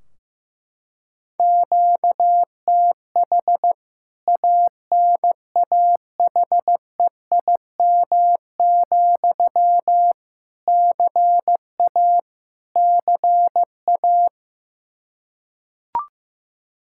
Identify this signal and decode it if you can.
Morse code, heard as a keyed tone, about 15 wpm, 700 Hz. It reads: QTH ANAHEIM, CA CA
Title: K1USN SST QSO Practice 15wpm